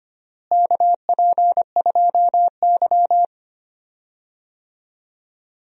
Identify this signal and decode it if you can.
Morse code, heard as a keyed tone, about 25 wpm, 700 Hz. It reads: KP2Y
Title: DXCC-100 Call Signs - 4 Characters - Repeated 2x - 25wpm